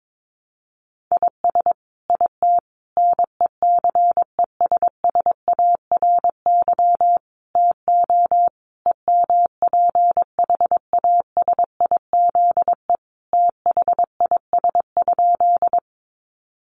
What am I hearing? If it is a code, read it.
Morse code, heard as a keyed tone, about 22 wpm, 700 Hz. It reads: IS IT NECESSARY TO EMPHASIZE THIS?
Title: Sentences from Top 1500 Words 22wpm